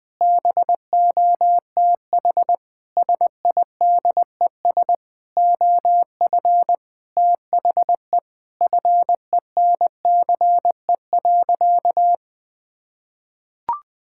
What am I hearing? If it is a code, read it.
Morse code, heard as a keyed tone, about 20 wpm, 700 Hz. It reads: BOTH SIDES OF THE FENCE.
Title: Sentences with 4-Word Phrases - Spoken Beginning - 20wpm